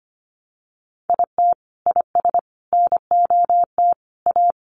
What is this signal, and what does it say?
Morse code, heard as a keyed tone, about 25 wpm, 700 Hz. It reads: IT IS NOT A
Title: Sentences from Top 800 Words 25wpm